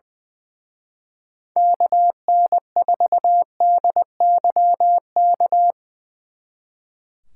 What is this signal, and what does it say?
Morse code, heard as a keyed tone, about 22 wpm, 700 Hz. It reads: KN4DYK
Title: Speed Racing - USA Call Signs 20wpm